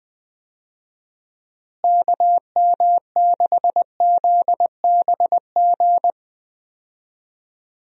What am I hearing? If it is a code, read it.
Morse code, heard as a keyed tone, about 20 wpm, 700 Hz. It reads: KM6ZBG
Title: USA Call Signs - 6 Characters - Repeated 2x - 20wpm